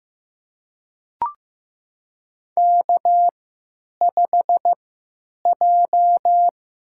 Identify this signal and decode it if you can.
Morse code, heard as a keyed tone, about 15 wpm, spaced as for 9 wpm, 700 Hz. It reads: K5J
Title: USA Call Signs - 3x Character Spacing - 15wpm